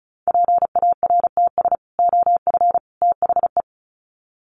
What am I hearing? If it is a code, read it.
Morse code, heard as a keyed tone, about 34 wpm, 700 Hz. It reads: PARTS OF THE
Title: Sets of 3 Words 35wpm